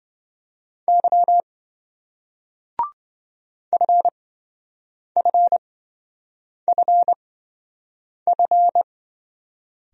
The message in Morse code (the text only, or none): Y FFFF